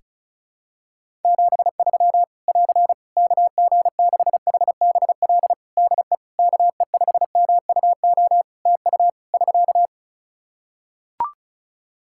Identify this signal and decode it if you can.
Morse code, heard as a keyed tone, about 34 wpm, 700 Hz. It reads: 73 <AR> KG6HBL DE KE5MUO TU <SK>
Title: Practice QSO - Exchange 4 - Ending - 35wpm